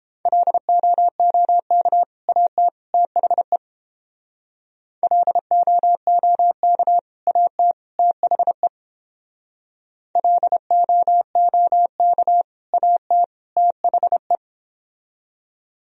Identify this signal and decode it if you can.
Morse code, heard as a keyed tone, about 31 wpm, 700 Hz. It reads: LOOK AT THE LOOK AT THE LOOK AT THE